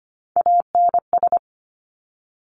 Morse code, heard as a keyed tone, about 25 wpm, 700 Hz. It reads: ANS
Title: Top Three Letter Combinations in Words 25wpm